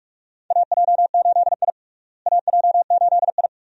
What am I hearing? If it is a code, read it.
Morse code, heard as a keyed tone, about 46 wpm, 700 Hz. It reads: AJ8I AJ8I
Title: USA Call Signs - 4 Characters - Repeated 2x - 45wpm